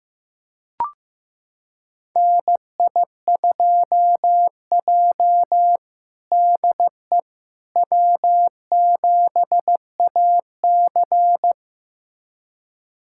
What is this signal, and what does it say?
Morse code, heard as a keyed tone, about 15 wpm, 700 Hz. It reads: NI2J DE W7AC
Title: USA Callsign DE Callsign - 4 Characters - 15wpm